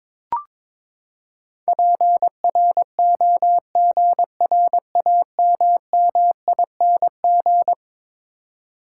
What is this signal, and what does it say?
Morse code, heard as a keyed tone, about 22 wpm, 700 Hz. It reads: PROGRAMMING
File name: practice / 22wpm / Eleven Letter Words - 22wpm